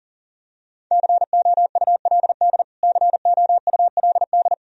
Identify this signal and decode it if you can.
Morse code, heard as a keyed tone, about 37 wpm, 700 Hz. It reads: COULD COULD
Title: Top 100 Words Repeated 40wpm